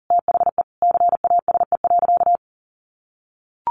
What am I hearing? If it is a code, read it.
Morse code, heard as a keyed tone, about 40 wpm, 700 Hz. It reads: THE CASE.
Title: Sentences from Top 1300 Words 40wpm